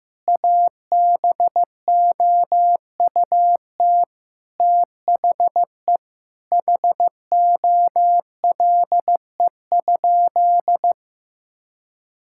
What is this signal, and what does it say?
Morse code, heard as a keyed tone, about 15 wpm, 700 Hz. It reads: ABOUT THE HOLE?
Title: Sentences from Top 1600 Words 15wpm